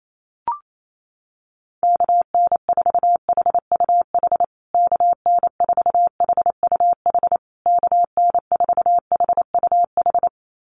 Morse code, heard as a keyed tone, about 28 wpm, 700 Hz. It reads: KN4HUH KN4HUH KN4HUH
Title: USA Call Signs Repeated 3x 28wpm